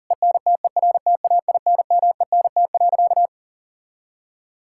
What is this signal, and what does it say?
Morse code, heard as a keyed tone, about 37 wpm, 690 Hz. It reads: ENTERTAINMENT.
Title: Sentences from Top 1000 Words - Review Part 6 - 40wpm